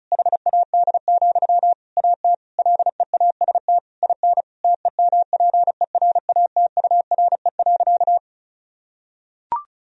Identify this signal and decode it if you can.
Morse code, heard as a keyed tone, about 36 wpm, 700 Hz. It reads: HAD, AT LEAST IN TEMPERATURE.
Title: Sentences from Top 1000 Words - Review Part 4 - 35wpm